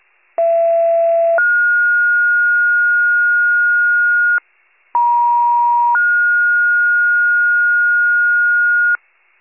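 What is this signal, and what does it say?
Two-tone page, heard as a one-second tone, then a three-second tone, about 670 then 1430 Hz, about 950 then 1430 Hz, with noise. It is heard from 0.4 to 9.0 s.
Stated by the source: County-wide paging alert, WCFM pager archive.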